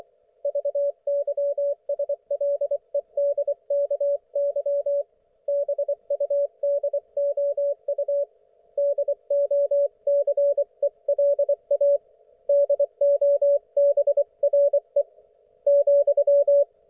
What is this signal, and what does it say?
Morse code, heard as a keyed tone, about 24 wpm, 570 Hz. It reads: VYSLEDKY BUDOU DOCELA DOBRE ,
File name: Archiv / CUC224 / ok1fhi cul224